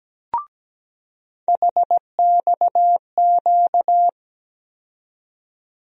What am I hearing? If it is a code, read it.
Morse code, heard as a keyed tone, about 17 wpm, 700 Hz. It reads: HXQ
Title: USA Call Sign Postfixes - 3 Letters 17wpm